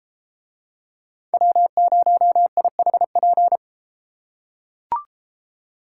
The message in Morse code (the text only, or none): W0IHP